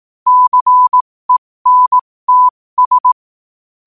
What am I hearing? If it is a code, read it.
Morse code, heard as a keyed tone, about 18 wpm, spaced as for 15 wpm, 1000 Hz. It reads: CENTS
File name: 15wpm